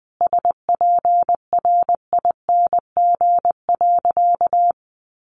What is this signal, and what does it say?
Morse code, heard as a keyed tone, about 20 wpm, 700 Hz. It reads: SPRING.
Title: Sentences from Top 1200 Words 20wpm